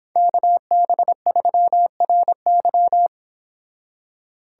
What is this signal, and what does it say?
Morse code, heard as a keyed tone, about 26 wpm, 700 Hz. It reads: KB3RY
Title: Speed Racing - USA Call Signs 22wpm